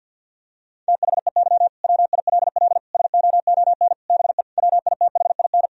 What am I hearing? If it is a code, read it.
Morse code, heard as a keyed tone, about 50 wpm, 700 Hz. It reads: THEY WILL SOON BE WITHIN
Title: Sentences from Top 1000 Words - Review Part 4 - 50wpm